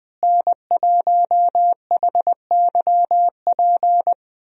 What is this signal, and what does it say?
Morse code, heard as a keyed tone, about 20 wpm, 700 Hz. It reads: N1HYP
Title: USA Call Signs - 5 Characters - Repeated 2x - 20wpm